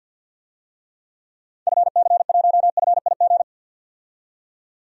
Morse code, heard as a keyed tone, about 52 wpm, 700 Hz. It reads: FC1FIG
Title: DXCC-100 Call Signs - 6 Characters - 50wpm